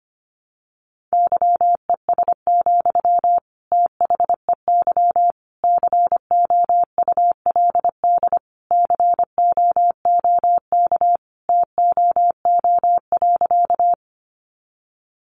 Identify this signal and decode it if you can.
Morse code, heard as a keyed tone, about 25 wpm, 700 Hz. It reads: YES, THEY COULD COOK TOO.
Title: Sentences from Top 700 Words 25wpm